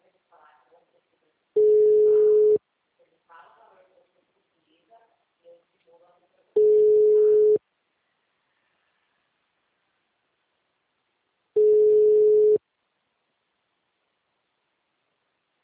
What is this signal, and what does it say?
Ringback tone (Europe, 425 Hz), with speech.